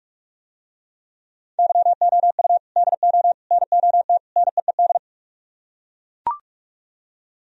Morse code, heard as a keyed tone, about 43 wpm, 700 Hz. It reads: YOU DO NOT NEED